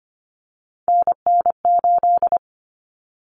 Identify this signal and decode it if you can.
Morse code, heard as a keyed tone, about 25 wpm, 700 Hz. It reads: NN8
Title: Common Cut Numbers - 3-Digit 25wpm